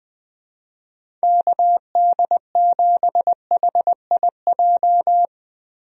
Morse code, heard as a keyed tone, about 20 wpm, 700 Hz. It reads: KD7HIJ
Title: USA Call Signs - 6 Characters - Repeated 2x - 20wpm